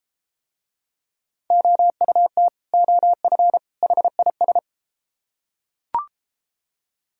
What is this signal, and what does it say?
Morse code, heard as a keyed tone, about 33 wpm, 700 Hz. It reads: OUT OF HIS